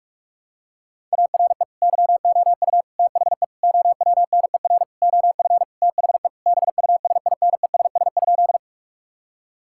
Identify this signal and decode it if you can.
Morse code, heard as a keyed tone, about 43 wpm, 700 Hz. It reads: ARE YOU THE OWNER OF THE BUSINESS?
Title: Sentences from Top 1600 Words 45wpm